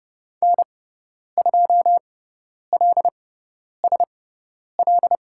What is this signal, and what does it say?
Morse code, heard as a keyed tone, about 30 wpm, spaced as for 10 wpm, 700 Hz. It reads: N2LSL
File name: Lesson 18g - Letter m - Call Signs 30wpm